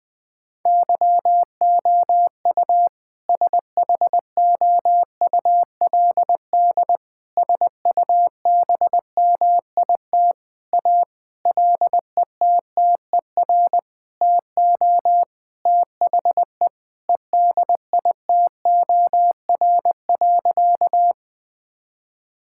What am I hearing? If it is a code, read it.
Morse code, heard as a keyed tone, about 20 wpm, 700 Hz. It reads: YOU SHOULD SUBMIT A LETTER TO THE EDITOR.
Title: Sentences with 6-Word Phrases - 20wpm